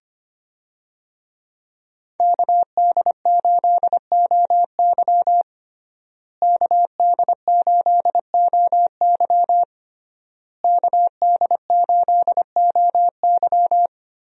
Morse code, heard as a keyed tone, about 25 wpm, 700 Hz. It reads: KD8OY KD8OY KD8OY
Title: USA Call Signs - 5 Characters - Repeated 3x - 25wpm